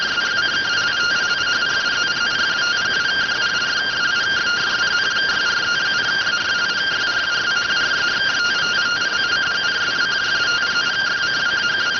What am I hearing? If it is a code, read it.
Two-tone teleprinter signal, about 170 Hz shift.
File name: findings / MR 2 Pel Sag T2 SSFSE 0009.gen-01 rx